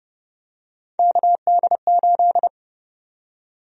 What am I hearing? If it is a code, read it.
Morse code, heard as a keyed tone, about 32 wpm, 700 Hz. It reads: KD8
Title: USA Call Sign Prefixes - Repeated 2x - 30wpm